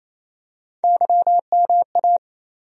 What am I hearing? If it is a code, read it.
Morse code, heard as a keyed tone, about 28 wpm, 700 Hz. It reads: YMA
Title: USA Call Sign Postfixes - 3 Letters 28wpm